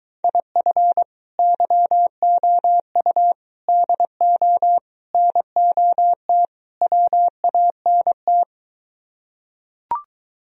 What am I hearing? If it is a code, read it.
Morse code, heard as a keyed tone, about 23 wpm, 700 Hz. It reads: IF YOU DO NOT WANT